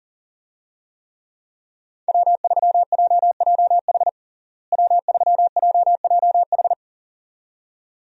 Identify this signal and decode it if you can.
Morse code, heard as a keyed tone, about 40 wpm, 700 Hz. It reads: W3JJH W3JJH
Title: USA Call Signs - 5 Characters - Repeated 2x - 40wpm